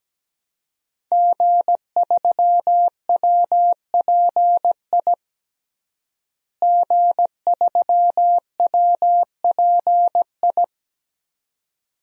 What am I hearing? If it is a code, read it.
Morse code, heard as a keyed tone, about 17 wpm, 700 Hz. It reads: G3WPI G3WPI
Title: DXCC-100 Call Signs - 5 Characters - Repeated 2x - 17wpm